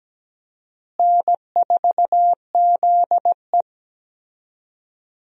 Morse code, heard as a keyed tone, about 17 wpm, 700 Hz. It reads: N4ZE
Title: USA Call Signs - 4 Characters - Repeated 3x - 17wpm